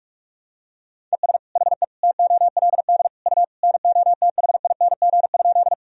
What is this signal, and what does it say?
Morse code, heard as a keyed tone, about 46 wpm, 700 Hz. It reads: ES HE TOLD U NOTHING?
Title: Sentences from Top 500 Words and Abbreviations 45wpm